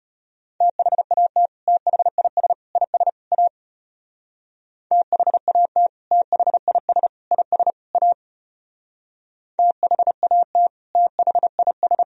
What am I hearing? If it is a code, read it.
Morse code, heard as a keyed tone, about 32 wpm, 700 Hz. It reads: THAT THIS IS A THAT THIS IS A THAT THIS